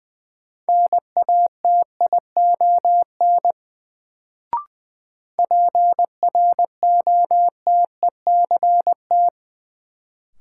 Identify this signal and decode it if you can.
Morse code, heard as a keyed tone, about 20 wpm, 700 Hz. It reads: NATION PROTECT
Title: Sentences from Top 900 Words - New Word Drill - 20wpm